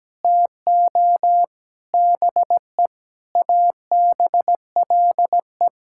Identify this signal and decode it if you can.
Morse code, heard as a keyed tone, about 17 wpm, 700 Hz. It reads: TO BE ABLE
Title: Sentences from Top 1000 Words - Review Part 6 - 17wpm